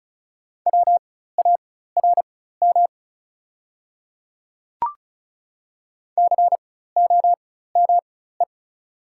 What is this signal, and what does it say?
Morse code, heard as a keyed tone, about 35 wpm, spaced as for 16 wpm, 700 Hz. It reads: WARM COME
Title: 4x Character Spacing - Top 500 Words - 35wpm